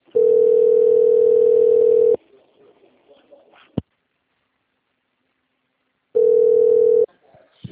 Ringback tone (North America), 440 + 480 Hz.
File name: out- - - - - .26636